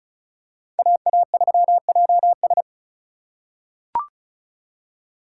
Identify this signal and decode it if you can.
Morse code, heard as a keyed tone, about 34 wpm, 700 Hz. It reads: AA3JS
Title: USA Call Signs - 5 Characters - 4x Character Spacing - 35wpm